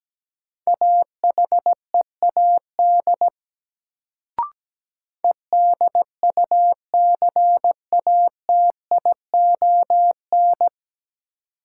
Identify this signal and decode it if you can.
Morse code, heard as a keyed tone, about 17 wpm, 700 Hz. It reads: AHEAD EDUCATION
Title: Sentences from Top 1100 Words - New Word Drill Alt - 17wpm